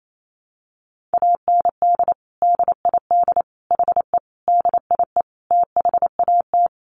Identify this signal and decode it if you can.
Morse code, heard as a keyed tone, about 28 wpm, 700 Hz. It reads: AND DID HE DIE THAT
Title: Sentences from Top 1300 Words 28wpm